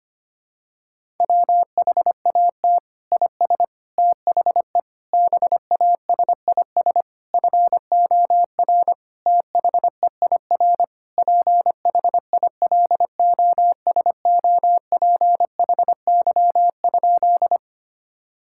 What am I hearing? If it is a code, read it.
Morse code, heard as a keyed tone, about 25 wpm, 700 Hz. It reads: WHAT IS THE BASIS FOR THEIR PHILOSOPHY?
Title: Sentences from Top 1000 Words - Review Part 6 - 25wpm